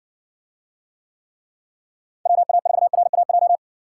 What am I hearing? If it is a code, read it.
Morse code, heard as a keyed tone, about 56 wpm, 700 Hz. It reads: WA4RAJ